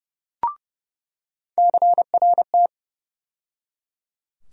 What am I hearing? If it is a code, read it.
Morse code, heard as a keyed tone, about 32 wpm, 700 Hz. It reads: CRT
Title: USA Call Sign Postfixes - 3 Letters 30wpm